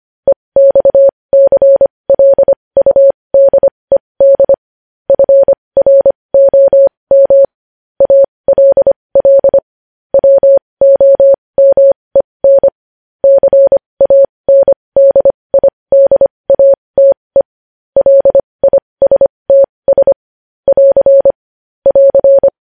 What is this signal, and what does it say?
Morse code, heard as a keyed tone, about 25 wpm, spaced as for 20 wpm, 550 Hz. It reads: EXCLUDED FROM ALL WOMEN CANDIDATE LISTS <AR> <AR>
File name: podcast / News headlines-1x-25x20